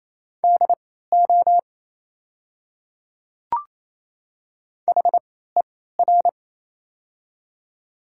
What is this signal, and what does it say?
Morse code, heard as a keyed tone, about 28 wpm, spaced as for 16 wpm, 700 Hz. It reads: DO HER
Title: 3x Character Spacing - Top 100 Words - 28wpm